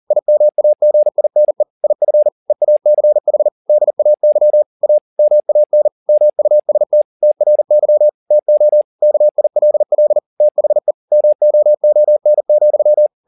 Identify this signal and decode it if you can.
Morse code, heard as a keyed tone, about 40 wpm, 600 Hz. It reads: IMAGINE IF EACH DAY A MAN MUST TRY TO KILL THE MOON,
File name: the old man and the sea 2 40wpm